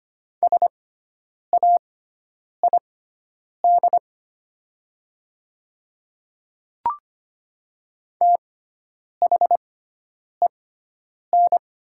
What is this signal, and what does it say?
Morse code, heard as a keyed tone, about 25 wpm, spaced as for 9 wpm, 700 Hz. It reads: SAID THEN
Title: 6x Character Spacing - Top 100 Words - 25wpm